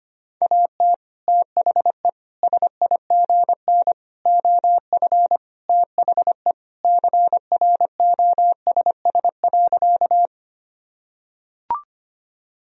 Morse code, heard as a keyed tone, about 25 wpm, 700 Hz. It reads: AT THE SIGN OF THE CROSS.
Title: Sentences from Top 1000 Words - Review Part 1 - 25wpm